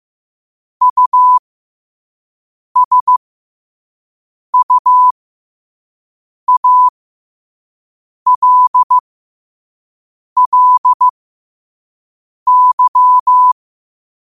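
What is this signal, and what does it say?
Morse code, heard as a keyed tone, about 15 wpm, spaced as for 5 wpm, 1000 Hz. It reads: USUALLY
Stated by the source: 5 wpm Morse code practice recording.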